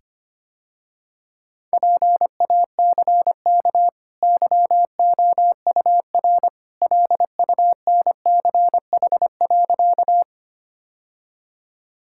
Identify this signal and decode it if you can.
Morse code, heard as a keyed tone, about 25 wpm, 700 Hz. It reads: PACK YOUR LUNCH.
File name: Easy Sentences with Hints - 25wpm